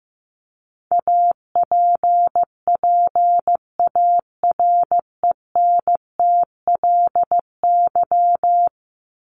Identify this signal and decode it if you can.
Morse code, heard as a keyed tone, about 15 wpm, 700 Hz. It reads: APPARENTLY
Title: LY Suffix - Top 100 Words with Base Words 15wpm